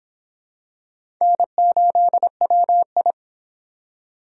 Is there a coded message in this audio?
Yes: Morse code from 1.2 to 3.1 s, heard as a keyed tone, about 23 wpm, 700 Hz. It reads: N8WI